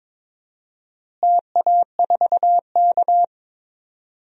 Morse code, heard as a keyed tone, about 22 wpm, 700 Hz. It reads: TA4K
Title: DXCC-100 Call Signs - 4 Characters - Repeated 2x - 22wpm